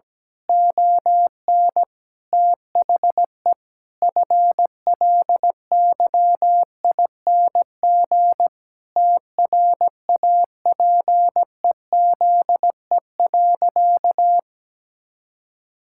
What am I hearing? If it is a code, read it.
Morse code, heard as a keyed tone, about 17 wpm, 700 Hz. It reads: ON THE FLYING TRAPEZE.
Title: Pangram Challenge v2 - 17wpm